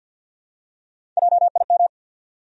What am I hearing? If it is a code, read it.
Morse code, heard as a keyed tone, about 51 wpm, 700 Hz. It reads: JIM
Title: USA Common Names - Max 4 Letters - 50wpm